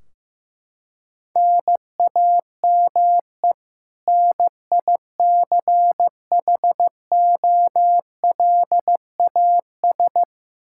Morse code, heard as a keyed tone, about 15 wpm, 700 Hz. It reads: NAME NICHOLAS
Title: K1USN SST QSO Practice 15wpm